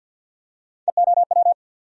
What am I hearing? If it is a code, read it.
Morse code, heard as a keyed tone, about 48 wpm, 700 Hz. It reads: EOW
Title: USA Call Sign Postfixes - 3 Letters 50wpm